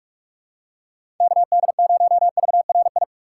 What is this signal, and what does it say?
Morse code, heard as a keyed tone, about 42 wpm, 700 Hz. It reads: KD0VRI
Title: USA Call Signs - 3x Character Spacing - 45wpm